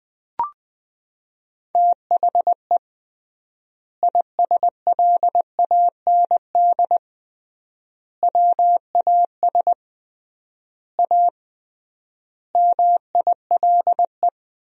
Morse code, heard as a keyed tone, about 20 wpm, 700 Hz. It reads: THE ISLAND WAS A MILE
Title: 3x Word-Spacing - Sentences from Top 500 Words - 20wpm